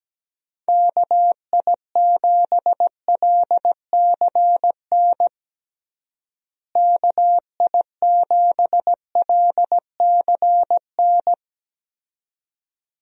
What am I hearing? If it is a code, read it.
Morse code, heard as a keyed tone, about 17 wpm, 700 Hz. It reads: KI7LCN KI7LCN
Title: USA Call Signs - 6 Characters - Repeated 2x - 17wpm